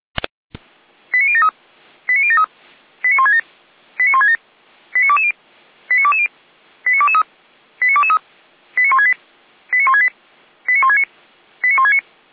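Five-tone selective call (ZVEI), heard as quick runs of five tones, tones of about 70 ms. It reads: 89073 89073 89167 89167 89220 89220 89223 89223 89168 89168 89169 89169
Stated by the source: Received on SDR; ZVEI five-tone selective call.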